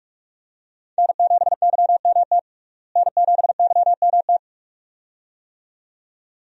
Morse code, heard as a keyed tone, about 41 wpm, 700 Hz. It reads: N7YMT N7YMT
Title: USA Call Signs - 5 Characters - Repeated 2x - 45wpm